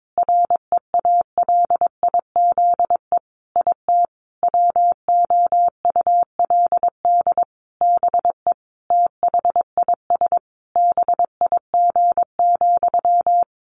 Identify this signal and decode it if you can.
Morse code, heard as a keyed tone, about 22 wpm, 700 Hz. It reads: REALIZE IT WOULD BE THIS BIG,
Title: A Sound Of Thunder 22wpm